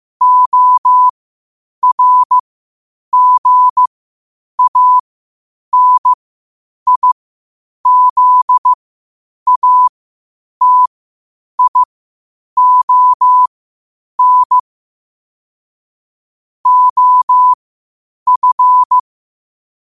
Morse code, heard as a keyed tone, about 15 wpm, spaced as for 8 wpm, 1000 Hz. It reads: ORGANIZATION OF
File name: qotdfiles / 8wpm